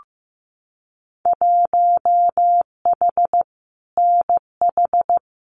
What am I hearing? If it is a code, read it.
Morse code, heard as a keyed tone, about 15 wpm, 700 Hz. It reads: 1H NH
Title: Winter Field Day Exchanges 15wpm